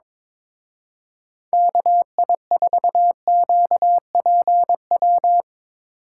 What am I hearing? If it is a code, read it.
Morse code, heard as a keyed tone, about 22 wpm, 700 Hz. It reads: KI4QPW
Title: Speed Racing - USA Call Signs 22wpm